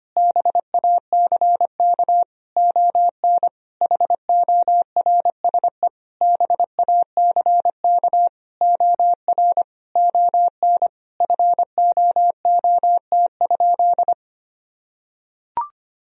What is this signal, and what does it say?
Morse code, heard as a keyed tone, about 25 wpm, 700 Hz. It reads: BACK ON HORSE BACK OR ON FOOT?
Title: Sentences from Top 1000 Words - Review Part 3 - 25wpm